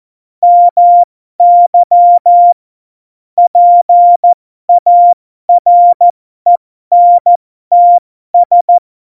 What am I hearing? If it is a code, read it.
Morse code, heard as a keyed tone, about 14 wpm, spaced as for 12 wpm, 700 Hz. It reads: MY PARENTS